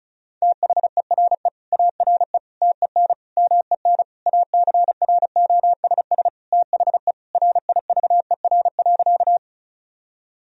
Morse code, heard as a keyed tone, about 36 wpm, 700 Hz. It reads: THERE ARE TEN MEN ACROSS THE RIVER.
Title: Sentences from Top 1000 Words - Review Part 2 - 35wpm